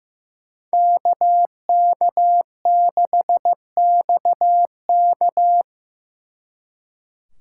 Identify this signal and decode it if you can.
Morse code, heard as a keyed tone, about 18 wpm, 700 Hz. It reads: KK6XK
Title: Speed Racing - USA 2x2 Call Signs 15wpm